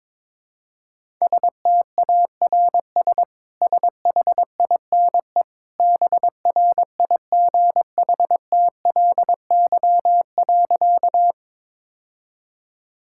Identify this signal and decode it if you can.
Morse code, heard as a keyed tone, about 22 wpm, 700 Hz. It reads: STARS SHINE BRIGHTLY.
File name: practice / 22wpm / Easy Sentences with Hints - 22wpm